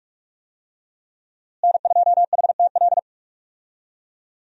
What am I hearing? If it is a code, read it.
Morse code, heard as a keyed tone, about 41 wpm, 700 Hz. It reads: N2HTL
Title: USA Call Signs - 5 Characters - Repeated 3x - 45wpm